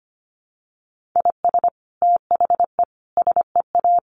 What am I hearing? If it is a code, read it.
Morse code, heard as a keyed tone, about 25 wpm, 700 Hz. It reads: IS THE SEA
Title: Sentences from Top 1400 Words 25wpm